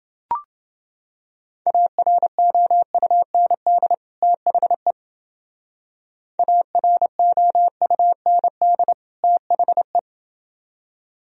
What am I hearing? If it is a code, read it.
Morse code, heard as a keyed tone, about 28 wpm, 700 Hz. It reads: AROUND THE AROUND THE